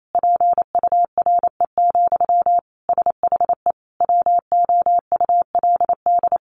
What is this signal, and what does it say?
Morse code, heard as a keyed tone, about 28 wpm, 700 Hz. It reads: PURE, SHE WOULD
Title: Sentences from Top 1500 Words 28wpm